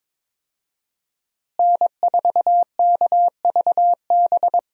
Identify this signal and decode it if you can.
Morse code, heard as a keyed tone, about 22 wpm, 700 Hz. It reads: N4KVB
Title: USA Call Signs - 5 Characters - Repeated 3x - 22wpm